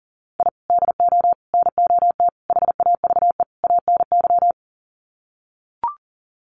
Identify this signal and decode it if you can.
Morse code, heard as a keyed tone, about 40 wpm, 700 Hz. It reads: I DO NOT HAVE ANY
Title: Sets of 5 Words 40wpm